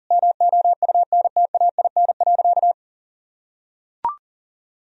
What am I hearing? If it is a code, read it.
Morse code, heard as a keyed tone, about 40 wpm, 700 Hz. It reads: MOUNTAIN.
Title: Sentences from Top 1000 Words - Review Part 1 - 40wpm